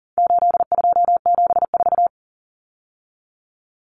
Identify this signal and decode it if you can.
Morse code, heard as a keyed tone, about 40 wpm, 700 Hz. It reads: 8274
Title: Numbers - 4-Digit 40wpm